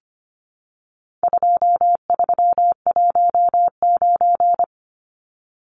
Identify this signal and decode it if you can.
Morse code, heard as a keyed tone, about 25 wpm, 700 Hz. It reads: 2319
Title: Numbers - 4-Digit 25wpm